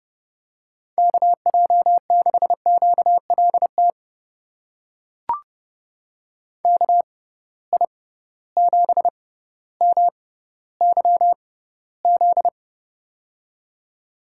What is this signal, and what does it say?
Morse code, heard as a keyed tone, about 32 wpm, spaced as for 16 wpm, 700 Hz. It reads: KJ6QLT KI7MYZ
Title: USA Call Signs - 6 Characters - 6x Character Spacing - 30wpm